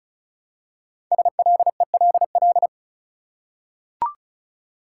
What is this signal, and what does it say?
Morse code, heard as a keyed tone, about 36 wpm, 700 Hz. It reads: SLELL
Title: Code Groups - 5 Letters - 35wpm